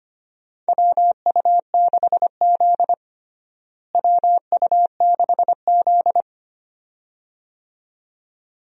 Morse code, heard as a keyed tone, about 25 wpm, 700 Hz. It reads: WU6Z WU6Z
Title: USA Call Signs - 4 Characters - Repeated 2x - 25wpm